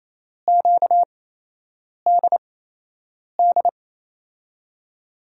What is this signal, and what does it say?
Morse code, heard as a keyed tone, about 28 wpm, spaced as for 8 wpm, 700 Hz. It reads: QDD
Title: USA Call Signs - 8x Character Spacing - 28wpm